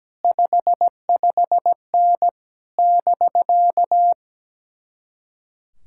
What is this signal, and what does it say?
Morse code, heard as a keyed tone, about 17 wpm, 700 Hz. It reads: 55N <BK>
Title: Speed-Racing Common QSO Phrases - 17wpm